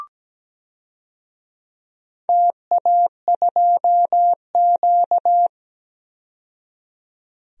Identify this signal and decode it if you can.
Morse code, heard as a keyed tone, about 17 wpm, 700 Hz. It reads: TA2Q
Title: DXCC-100 Call Signs - 4 Characters - 17wpm